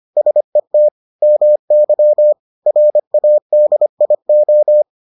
Morse code, heard as a keyed tone, about 25 wpm, 600 Hz. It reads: SET MY RADIO